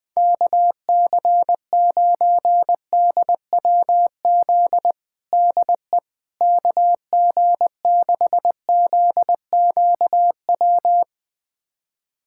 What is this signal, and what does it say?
Morse code, heard as a keyed tone, about 19 wpm, 700 Hz. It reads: KC9DWZ DE KG6ZQW
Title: USA Callsign DE Callsign - 6 Characters - 20wpm